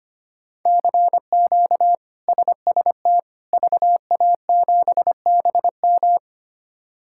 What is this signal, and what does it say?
Morse code, heard as a keyed tone, about 25 wpm, 700 Hz. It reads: CQ SST VA7BM
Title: K1USN SST QSO Practice - 25wpm